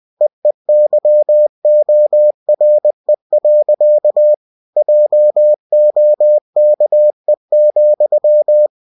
Morse code, heard as a keyed tone, about 20 wpm, 600 Hz. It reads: EEYORE. JOKE,
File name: cw-books / winnie-the-pooh / winnie-20wpm-page-0047